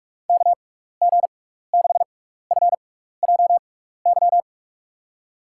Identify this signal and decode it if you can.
Morse code, heard as a keyed tone, about 42 wpm, spaced as for 15 wpm, 700 Hz. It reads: KG6FJY
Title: USA Call Signs - 6 Characters - 6x Character Spacing - 45wpm